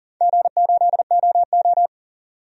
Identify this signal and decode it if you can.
Morse code, heard as a keyed tone, about 40 wpm, 700 Hz. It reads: G8OO
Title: DXCC-100 Call Signs - 4 Characters - 40wpm